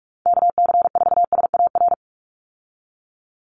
Morse code, heard as a keyed tone, about 44 wpm, 700 Hz. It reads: KC4SAR
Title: USA Call Signs 45wpm